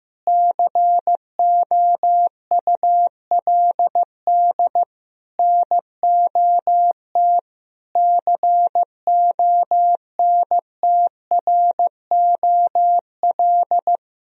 Morse code, heard as a keyed tone, about 16 wpm, 700 Hz. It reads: COULD NOT CONTROL
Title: Sentences from Top 1000 Words - Review Part 3 - 15wpm